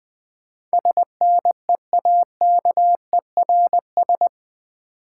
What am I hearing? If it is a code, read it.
Morse code, heard as a keyed tone, about 20 wpm, 690 Hz. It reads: SNEAKERS
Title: Easy Words with Hints - 20wpm